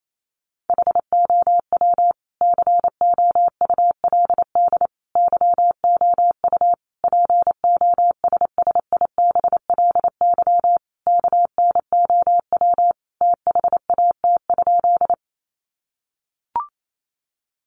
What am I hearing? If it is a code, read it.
Morse code, heard as a keyed tone, about 28 wpm, 700 Hz. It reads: HOW COULD YOU POSSIBLY KNOW THAT?
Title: Sentences from Top 1300 Words 28wpm